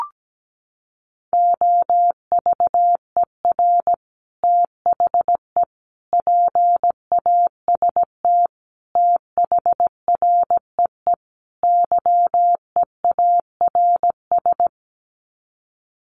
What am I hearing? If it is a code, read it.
Morse code, heard as a keyed tone, about 17 wpm, 700 Hz. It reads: OVER THE PAST THREE YEARS
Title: Sets of 5 Words Encore2 17wpm